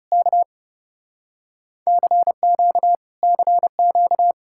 Morse code, heard as a keyed tone, about 37 wpm, 700 Hz. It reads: K CQ CQ
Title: Speed-Racing Common QSO Phrases - 30wpm